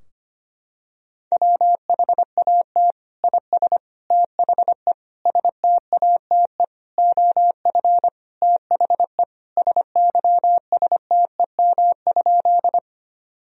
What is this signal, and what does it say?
Morse code, heard as a keyed tone, about 25 wpm, 700 Hz. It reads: WHAT IS THE STATE OF THE SYSTEM?
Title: Sentences from Top 1000 Words - Review Part 3 - 25wpm